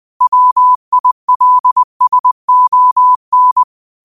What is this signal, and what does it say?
Morse code, heard as a keyed tone, about 20 wpm, 1000 Hz. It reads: WILSON
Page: Quotes for Tue, 12 Aug 2025 in Morse Code at 20 words per minute.